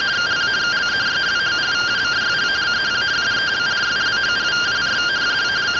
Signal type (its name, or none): two-tone teleprinter signal